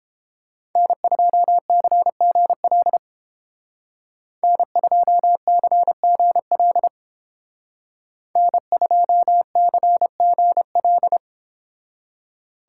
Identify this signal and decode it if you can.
Morse code, heard as a keyed tone, about 29 wpm, 700 Hz. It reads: N2CGL N2CGL N2CGL